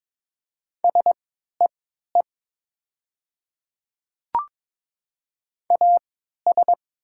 Morse code, heard as a keyed tone, about 21 wpm, spaced as for 12 wpm, 700 Hz. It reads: SEE AS
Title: 3x Character Spacing - Top 200 Words - 22wpm